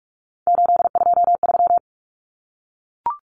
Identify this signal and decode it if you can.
Morse code, heard as a keyed tone, about 44 wpm, 700 Hz. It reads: 823
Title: Common Cut Numbers - 3-Digit 45wpm